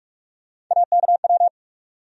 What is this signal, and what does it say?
Morse code, heard as a keyed tone, about 46 wpm, 700 Hz. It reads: AKW
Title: USA Call Sign Postfixes - 3 Letters 45wpm